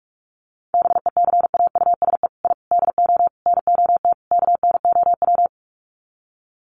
Morse code, heard as a keyed tone, about 44 wpm, 700 Hz. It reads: BECAUSE I DO NOT KNOW
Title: Sets of 5 Words Encore2 45wpm